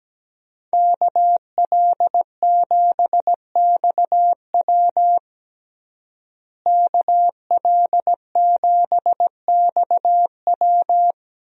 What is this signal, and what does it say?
Morse code, heard as a keyed tone, about 17 wpm, 700 Hz. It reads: KL7XW KL7XW
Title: DXCC-100 Call Signs - 5 Characters - Repeated 2x - 17wpm